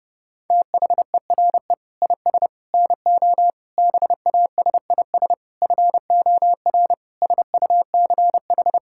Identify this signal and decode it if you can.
Morse code, heard as a keyed tone, about 30 wpm, 700 Hz. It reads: THERE IS NO BASIS FOR SUCH
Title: Sentences from Top 1000 Words - Review Part 5 - 30wpm